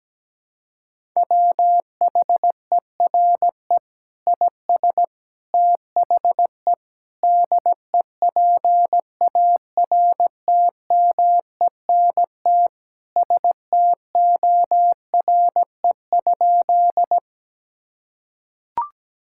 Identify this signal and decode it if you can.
Morse code, heard as a keyed tone, about 17 wpm, 700 Hz. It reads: WHERE IS THE DEPARTMENT STORE?
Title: Sentences from Top 1000 Words - Review Part 5 - 17wpm